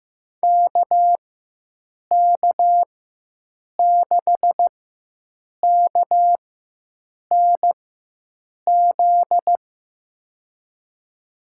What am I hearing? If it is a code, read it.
Morse code, heard as a keyed tone, about 16 wpm, spaced as for 7 wpm, 700 Hz. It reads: KK6KNZ
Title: USA Call Signs - 4x Character Spacing - 15wpm